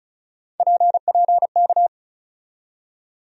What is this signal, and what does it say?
Morse code, heard as a keyed tone, about 34 wpm, 700 Hz. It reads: PPK
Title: USA Call Sign Postfixes - 3 Letters 35wpm